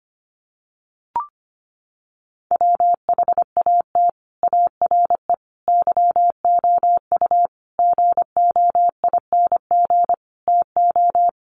Morse code, heard as a keyed tone, about 25 wpm, 700 Hz. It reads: WHAT ARE YOU GOING TO
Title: Sets of 5 Words 25wpm